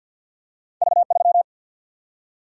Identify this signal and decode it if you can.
Morse code, heard as a keyed tone, about 48 wpm, 700 Hz. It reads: V 3
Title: Single-Letter Single-Number 50wpm